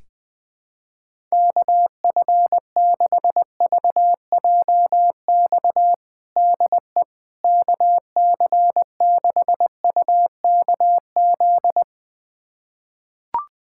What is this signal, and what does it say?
Morse code, heard as a keyed tone, about 20 wpm, 700 Hz. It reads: KF6VJX DE KC6UKZ
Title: USA Callsign DE Callsign - 6 Characters - 20wpm